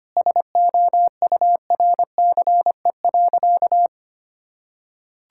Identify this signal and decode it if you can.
Morse code, heard as a keyed tone, about 25 wpm, 700 Hz. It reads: SOURCE.
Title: Sentences from Top 1000 Words - Review Part 5 - 25wpm